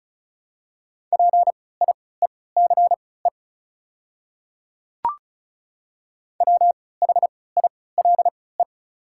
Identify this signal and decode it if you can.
Morse code, heard as a keyed tone, about 34 wpm, spaced as for 20 wpm, 700 Hz. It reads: PIECE WHILE
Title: 3x Character Spacing - Top 500 Words - 35wpm